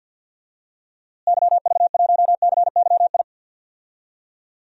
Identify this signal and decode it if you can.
Morse code, heard as a keyed tone, about 49 wpm, 700 Hz. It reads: YV1CYI
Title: DXCC-100 Call Signs - 6 Characters - Repeated 2x - 50wpm